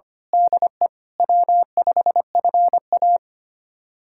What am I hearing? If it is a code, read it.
Morse code, heard as a keyed tone, about 25 wpm, 700 Hz. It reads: DE W5FA
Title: USA Callsign DE Callsign - 4 Characters - 25wpm